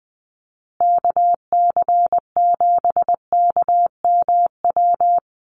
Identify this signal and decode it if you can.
Morse code, heard as a keyed tone, about 20 wpm, 700 Hz. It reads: KC7KMW
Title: USA Call Signs Repeated 3x 20wpm